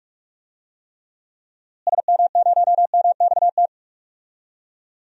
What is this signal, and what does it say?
Morse code, heard as a keyed tone, about 45 wpm, 700 Hz. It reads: SM0MXT
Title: DXCC-100 Call Signs - 6 Characters - Repeated 3x - 45wpm